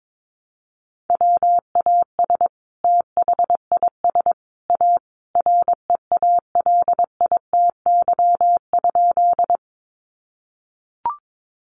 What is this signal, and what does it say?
Morse code, heard as a keyed tone, about 22 wpm, 700 Hz. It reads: WAS THIS A REALITY?
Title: Sentences from Top 1200 Words 22wpm